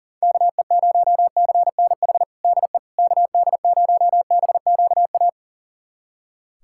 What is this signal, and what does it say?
Morse code, heard as a keyed tone, about 40 wpm, 700 Hz. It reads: KE0CNH DE KD0BQA
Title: USA Callsign DE Callsign - 6 Characters - 40wpm